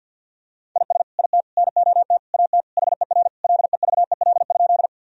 Morse code, heard as a keyed tone, about 50 wpm, 700 Hz. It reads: IS IT NOT AT HER LEVEL?
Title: Sentences from Top 1000 Words - Review Part 4 - 50wpm